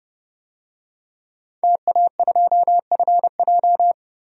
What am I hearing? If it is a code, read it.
Morse code, heard as a keyed tone, about 30 wpm, 700 Hz. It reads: TA2FJ
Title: DXCC-100 Call Signs - 5 Characters - 30wpm